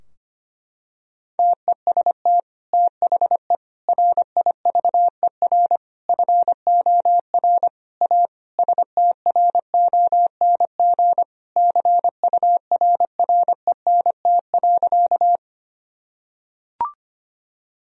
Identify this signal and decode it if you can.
Morse code, heard as a keyed tone, about 25 wpm, 700 Hz. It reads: TEST THE RIVER FOR A STRONG CURRENT.
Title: Sentences from Top 1000 Words - Review Part 4 - 25wpm